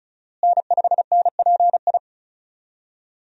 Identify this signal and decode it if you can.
Morse code, heard as a keyed tone, about 34 wpm, 700 Hz. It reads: N5NPI
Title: USA Call Signs - 5 Characters - Repeated 2x - 35wpm